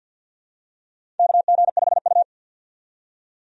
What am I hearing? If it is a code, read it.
Morse code, heard as a keyed tone, about 49 wpm, 700 Hz. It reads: KG5U
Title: USA Call Signs - 4 Characters - 50wpm